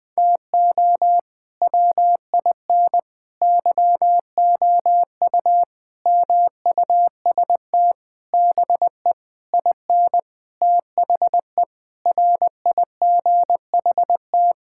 Morse code, heard as a keyed tone, about 20 wpm, 700 Hz. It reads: TO WIN YOU MUST BE IN THE RIGHT
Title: Sentences with 9-Word Phrases - 20wpm